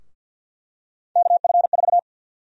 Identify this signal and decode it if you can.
Morse code, heard as a keyed tone, about 51 wpm, 700 Hz. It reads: KF4
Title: USA Call Sign Prefixes - Repeated 2x - 50wpm